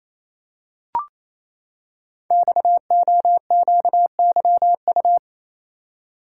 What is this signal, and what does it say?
Morse code, heard as a keyed tone, about 29 wpm, 700 Hz. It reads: XOQYU
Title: Code Groups - 5 Letters - 28wpm